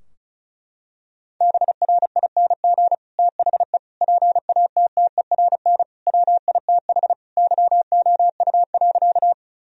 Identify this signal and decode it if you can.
Morse code, heard as a keyed tone, about 34 wpm, 700 Hz. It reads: BRING THE PATTERN WITH YOU.
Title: Sentences from Top 1000 Words - Review Part 2 - 35wpm